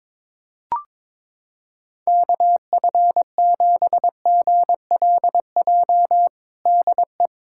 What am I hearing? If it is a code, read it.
Morse code, heard as a keyed tone, about 22 wpm, 700 Hz. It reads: KF7GLJ DE
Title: USA Callsign DE Callsign - 6 Characters - 22wpm